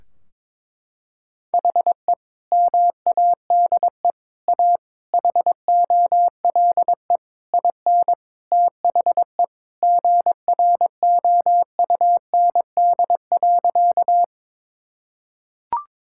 Morse code, heard as a keyed tone, about 21 wpm, 700 Hz. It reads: HE MADE A HOLE IN THE GROUND.
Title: Sentences from Top 1600 Words 22wpm